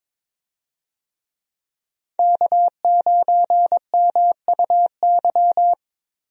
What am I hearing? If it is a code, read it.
Morse code, heard as a keyed tone, about 21 wpm, 700 Hz. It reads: K9MUY
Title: USA Call Signs - 5 Characters - Repeated 3x - 22wpm